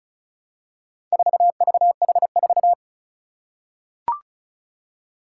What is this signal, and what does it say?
Morse code, heard as a keyed tone, about 34 wpm, 700 Hz. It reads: 4VH4
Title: Four Challenging Letter-Number Combinations 35wpm